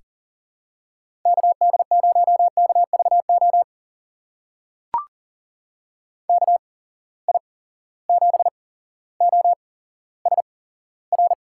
Morse code, heard as a keyed tone, about 40 wpm, spaced as for 17 wpm, 700 Hz. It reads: KD0KVO KI7OSR
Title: USA Call Signs - 6 Characters - 8x Character Spacing - 40wpm